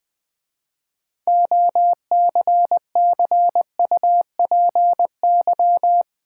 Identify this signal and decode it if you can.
Morse code, heard as a keyed tone, about 21 wpm, 700 Hz. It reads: OCCUPY
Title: 6 Letter Words - First 3 Letters Given - 20wpm